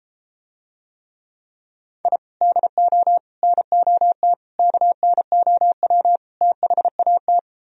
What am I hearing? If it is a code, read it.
Morse code, heard as a keyed tone, about 32 wpm, 700 Hz. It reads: I DO NOT KNOW THAT